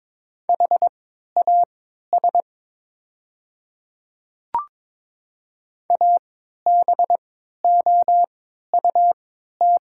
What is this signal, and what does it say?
Morse code, heard as a keyed tone, about 22 wpm, spaced as for 12 wpm, 700 Hz. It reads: HAS ABOUT
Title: 3x Character Spacing - Top 100 Words - 22wpm